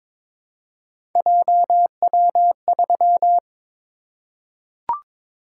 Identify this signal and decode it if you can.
Morse code, heard as a keyed tone, about 22 wpm, 700 Hz. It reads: JW3
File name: practice / 22wpm / Three Letter-Number Combinations - 22wpm